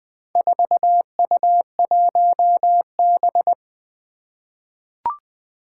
Morse code, heard as a keyed tone, about 20 wpm, 700 Hz. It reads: 4U1B
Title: Four Challenging Letter-Number Combinations 20wpm